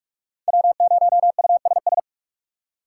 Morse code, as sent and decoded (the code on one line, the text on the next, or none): .-- ----- ..- ... ...
W0USS